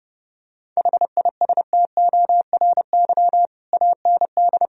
Morse code, heard as a keyed tone, about 32 wpm, 700 Hz. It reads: HISTORY AND
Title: Sentences from Top 1000 Words - Review Part 2 - 30wpm